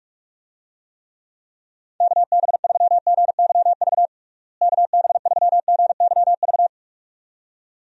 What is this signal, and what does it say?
Morse code, heard as a keyed tone, about 48 wpm, 700 Hz. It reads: KB3GYV KB3GYV
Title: USA Call Signs - 6 Characters - Repeated 2x - 45wpm